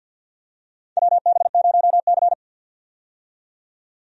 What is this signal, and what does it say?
Morse code, heard as a keyed tone, about 50 wpm, 700 Hz. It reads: WB0C
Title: USA Call Signs - 4 Characters - 50wpm